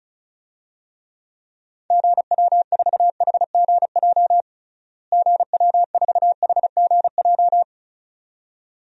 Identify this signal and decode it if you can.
Morse code, heard as a keyed tone, about 34 wpm, 700 Hz. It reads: GW4HGJ GW4HGJ
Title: DXCC-100 Call Signs - 6 Characters - Repeated 2x - 35wpm